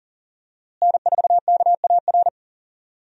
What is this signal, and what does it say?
Morse code, heard as a keyed tone, about 40 wpm, 700 Hz. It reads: N4KAR
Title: USA Call Signs - 5 Characters - Repeated 2x - 40wpm